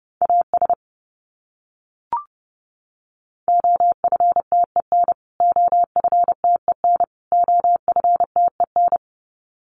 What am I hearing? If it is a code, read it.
Morse code, heard as a keyed tone, about 29 wpm, 700 Hz. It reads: AS OFTEN OFTEN OFTEN
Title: Top 500 Words Repeated 30wpm